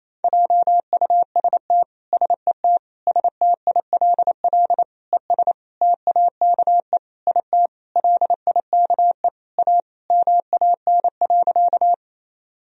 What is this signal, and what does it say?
Morse code, heard as a keyed tone, about 28 wpm, 700 Hz. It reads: JUST SET STILL ES TAKE IT LIKE A MAN.
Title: Sentences from Top 300 Words and Abbreviations 28wpm